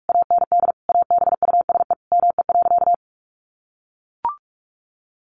Morse code, heard as a keyed tone, about 44 wpm, 700 Hz. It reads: AND ABUSE ME.
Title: Sentences from Top 1300 Words 45wpm